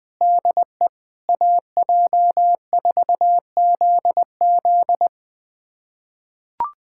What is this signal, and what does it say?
Morse code, heard as a keyed tone, about 21 wpm, 700 Hz. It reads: DE AJ4ZZ
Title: USA Callsign DE Callsign - 5 Characters - 20wpm